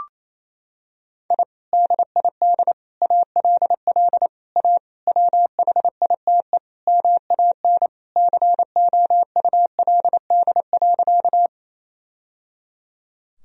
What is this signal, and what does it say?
Morse code, heard as a keyed tone, about 28 wpm, 700 Hz. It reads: I DID ALL A WHITE MAN COULD.
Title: Sentences from Top 1000 Words - Review Part 1 - 28wpm